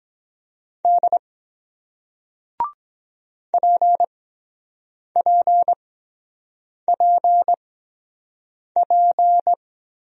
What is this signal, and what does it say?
Morse code, heard as a keyed tone, about 23 wpm, spaced as for 7 wpm, 700 Hz. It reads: D PPPP